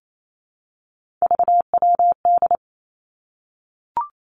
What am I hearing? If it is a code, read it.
Morse code, heard as a keyed tone, about 28 wpm, 700 Hz. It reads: VWD
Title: Three Challenging Letter-Number Combinations 28wpm